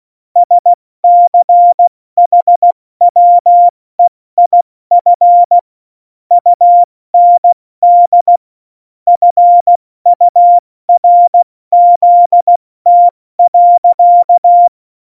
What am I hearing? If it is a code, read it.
Morse code, heard as a keyed tone, about 16 wpm, 700 Hz. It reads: SCHWEIF UND FURZT.
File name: Morsekurs / Witze / Witz 19 Mai, 2020 70 F80